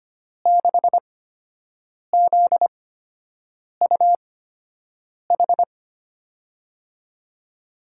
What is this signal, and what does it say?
Morse code, heard as a keyed tone, about 25 wpm, spaced as for 7 wpm, 700 Hz. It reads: 6ZUH
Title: USA Call Signs - 6 Characters - 8x Character Spacing - 25wpm